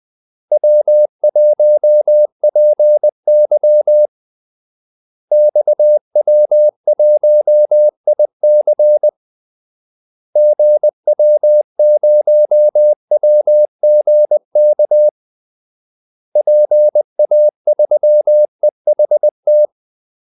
Morse code, heard as a keyed tone, about 20 wpm, 600 Hz. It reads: W1PY XW1IC GW0WGK PA3EHT
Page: Kaldesignaler 20 wpm